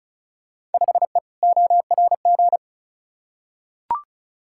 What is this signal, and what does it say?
Morse code, heard as a keyed tone, about 36 wpm, 700 Hz. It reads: 5E ORG
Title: Field Day Exchanges Encore - 35wpm